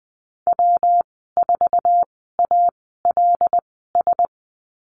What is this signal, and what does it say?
Morse code, heard as a keyed tone, about 20 wpm, spaced as for 15 wpm, 700 Hz. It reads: W4ALS
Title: 147 Lesson 24g - Letter y - Call Signs 20wpm